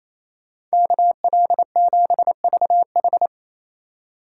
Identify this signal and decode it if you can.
Morse code, heard as a keyed tone, about 35 wpm, 700 Hz. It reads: KL7VH
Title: Speed Racing - USA 2x2 Call Signs 28wpm